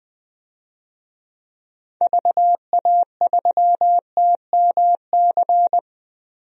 Morse code, heard as a keyed tone, about 20 wpm, 700 Hz. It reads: VA3TMC
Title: DXCC-100 Call Signs - 6 Characters - Repeated 2x - 20wpm